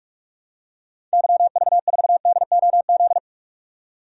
Morse code, heard as a keyed tone, about 46 wpm, 700 Hz. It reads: YV4DOZ
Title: DXCC-100 Call Signs - 6 Characters - 45wpm